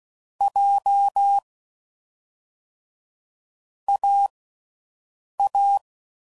Morse code, heard as a keyed tone, about 16 wpm, 800 Hz. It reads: J A A
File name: Combinations / Difficult Combos / AWJ1